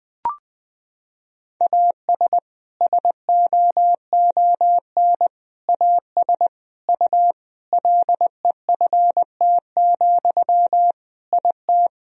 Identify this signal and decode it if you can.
Morse code, heard as a keyed tone, about 20 wpm, 700 Hz. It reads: AS SOON AS U LEFT, IT
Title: Sentences from Top 400 Words and Abbreviations 20wpm